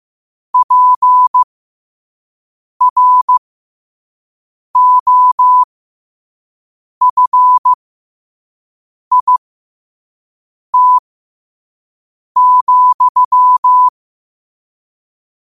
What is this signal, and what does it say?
Morse code, heard as a keyed tone, about 15 wpm, spaced as for 5 wpm, 1000 Hz. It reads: PROFIT,
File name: qotdfiles / 5wpm